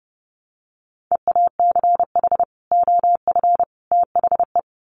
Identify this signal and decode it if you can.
Morse code, heard as a keyed tone, about 30 wpm, 700 Hz. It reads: EACH OF THE
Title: Sets of 3 Words 30wpm